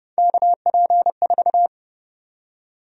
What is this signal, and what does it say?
Morse code, heard as a keyed tone, about 31 wpm, 700 Hz. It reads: KP4
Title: USA Call Sign Prefixes - 30wpm